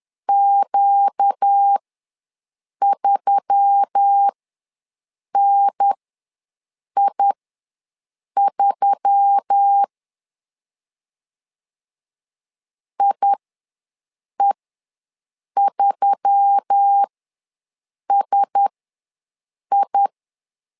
Morse code, heard as a keyed tone, about 10 wpm, spaced as for 6 wpm, 800 Hz. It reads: Q3NI3 IE3SI
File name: LEZIONI / 4 Q1S3 ws 10wpm 2437